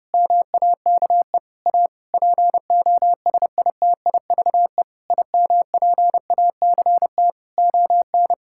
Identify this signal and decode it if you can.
Morse code, heard as a keyed tone, about 30 wpm, 700 Hz. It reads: MAKE A POSITIVE IMPACT ON
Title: Sentences from Top 1000 Words - Review Part 6 - 30wpm